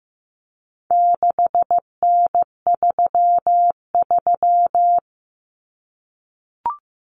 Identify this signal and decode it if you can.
Morse code, heard as a keyed tone, about 15 wpm, 700 Hz. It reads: 6N33
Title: Common Cut Numbers - 4-Digit 15wpm